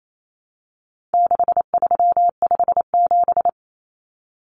Morse code, heard as a keyed tone, about 28 wpm, 700 Hz. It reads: 6357
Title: Common Cut Numbers - 4-Digit 28wpm